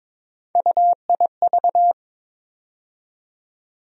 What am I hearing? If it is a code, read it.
Morse code, heard as a keyed tone, about 22 wpm, 700 Hz. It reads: UIV
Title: USA Call Sign Postfixes - 3 Letters 22wpm